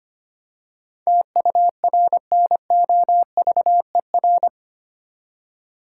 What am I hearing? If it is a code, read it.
Morse code, heard as a keyed tone, about 25 wpm, 700 Hz. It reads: TURNOVER
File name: Eight Letter Words - 25wpm